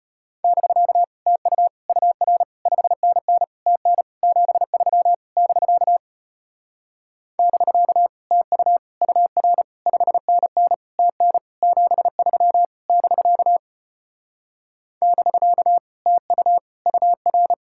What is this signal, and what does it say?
Morse code, heard as a keyed tone, about 33 wpm, 700 Hz. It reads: <BK> TU UR 5NN TN 73 <BK> <BK> TU UR 5NN TN 73 <BK> <BK> TU UR